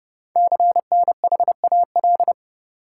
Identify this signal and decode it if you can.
Morse code, heard as a keyed tone, about 29 wpm, 690 Hz. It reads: CNHAL
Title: Code Groups - 5 Letters - 30wpm